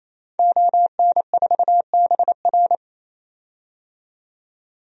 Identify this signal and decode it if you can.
Morse code, heard as a keyed tone, about 28 wpm, 700 Hz. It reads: ON4BR
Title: DXCC-100 Call Signs - 5 Characters - 28wpm